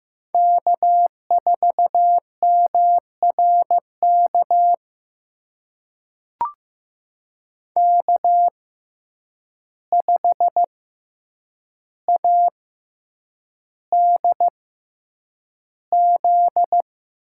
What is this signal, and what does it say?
Morse code, heard as a keyed tone, about 15 wpm, spaced as for 8 wpm, 700 Hz. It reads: K4MRK K5ADZ
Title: USA Call Signs - 5 Characters - 6x Character Spacing - 15wpm